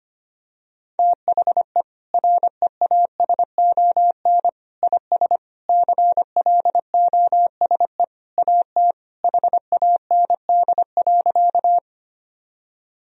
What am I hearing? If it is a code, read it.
Morse code, heard as a keyed tone, about 25 wpm, 700 Hz. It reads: THE REASON IS CLOSE AT HAND.
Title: Sentences from Top 1000 Words - Review Part 1 - 25wpm